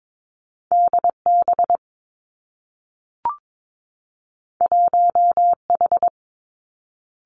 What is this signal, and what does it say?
Morse code, heard as a keyed tone, about 22 wpm, 700 Hz. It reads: D B 1 H
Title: Two Challenging Letter-Number Combinations 22wpm